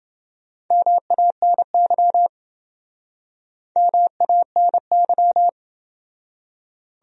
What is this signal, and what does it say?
Morse code, heard as a keyed tone, about 27 wpm, 700 Hz. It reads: MANY MANY